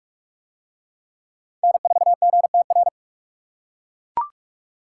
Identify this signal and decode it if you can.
Morse code, heard as a keyed tone, about 41 wpm, 700 Hz. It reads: N4GTR